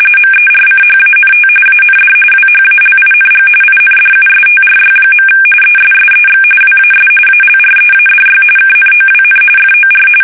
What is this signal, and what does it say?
RTTY, about 45.45 baud. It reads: MERCHANT AND WHALING INDUSTRY. ESSEX WAS ALREADY KNOWN FOR
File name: fsk850 / usnavy2